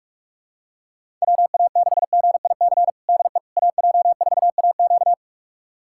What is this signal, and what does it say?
Morse code, heard as a keyed tone, about 46 wpm, 700 Hz. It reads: WA6GIC DE AJ4AQ
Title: USA Callsign DE Callsign 45wpm